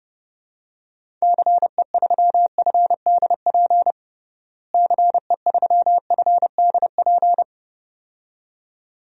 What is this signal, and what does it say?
Morse code, heard as a keyed tone, about 30 wpm, 700 Hz. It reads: CE3FDP CE3FDP
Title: DXCC-100 Call Signs - 6 Characters - Repeated 2x - 30wpm